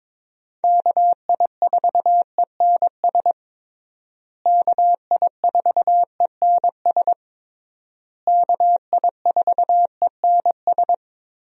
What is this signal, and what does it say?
Morse code, heard as a keyed tone, about 22 wpm, 700 Hz. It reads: KI4ENS KI4ENS KI4ENS
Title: USA Call Signs - 6 Characters - Repeated 3x - 22wpm